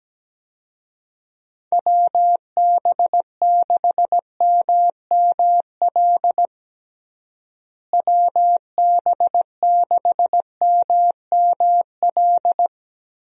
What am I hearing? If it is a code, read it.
Morse code, heard as a keyed tone, about 17 wpm, 700 Hz. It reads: WB6MML WB6MML
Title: USA Call Signs - 6 Characters - Repeated 3x - 17wpm